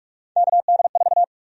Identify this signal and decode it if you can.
Morse code, heard as a keyed tone, about 46 wpm, 700 Hz. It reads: KD4
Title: USA Call Sign Prefixes - 45wpm